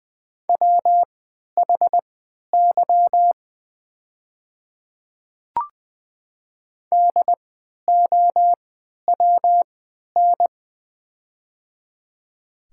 Morse code, heard as a keyed tone, about 20 wpm, spaced as for 11 wpm, 700 Hz. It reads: WHY DOWN
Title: 3x Character Spacing - Top 200 Words - 20wpm